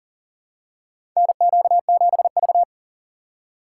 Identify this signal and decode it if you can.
Morse code, heard as a keyed tone, about 40 wpm, 700 Hz. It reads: NQ7V
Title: USA Call Signs - 4 Characters - 6x Character Spacing - 40wpm